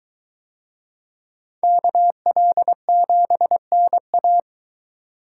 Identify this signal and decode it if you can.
Morse code, heard as a keyed tone, about 22 wpm, 700 Hz. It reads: KL7NA